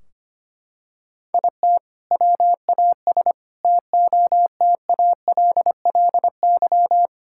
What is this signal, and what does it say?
Morse code, heard as a keyed tone, about 25 wpm, 700 Hz. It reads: IT WAS TOTALLY
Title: Sentences from Top 1600 Words 25wpm